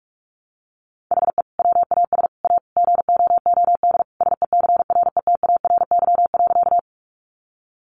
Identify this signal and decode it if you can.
Morse code, heard as a keyed tone, about 44 wpm, 700 Hz. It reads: HE WAS A GOOD SECRETARY.
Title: Sentences from Top 1300 Words 45wpm